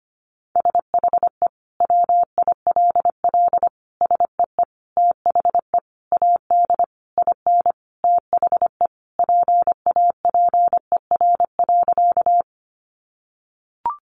Morse code, heard as a keyed tone, about 25 wpm, 700 Hz. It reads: SHE WILL SEE THE AD IN THE PAPER.
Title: Sentences from Top 1300 Words 25wpm